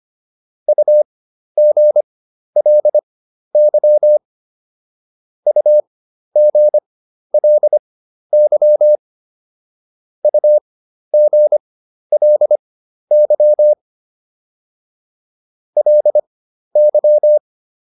Morse code, heard as a keyed tone, about 25 wpm, spaced as for 12 wpm, 600 Hz. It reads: UGLY UGLY UGLY LY